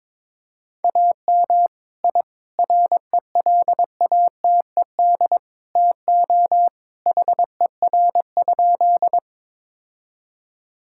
Morse code, heard as a keyed tone, about 23 wpm, 700 Hz. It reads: AM I RELATED TO HER?
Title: Sentences from Top 1000 Words - Review Part 5 - 22wpm